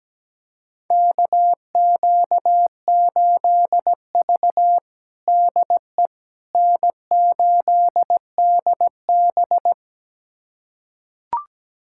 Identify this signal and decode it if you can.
Morse code, heard as a keyed tone, about 17 wpm, 700 Hz. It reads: KQ8V DE N8DB
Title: USA Callsign DE Callsign - 4 Characters - 17wpm